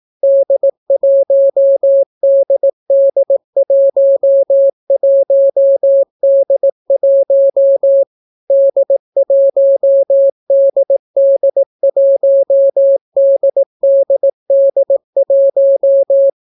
Morse code, heard as a keyed tone, about 18 wpm, 550 Hz. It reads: D1DD11D1 D1DD1DDD1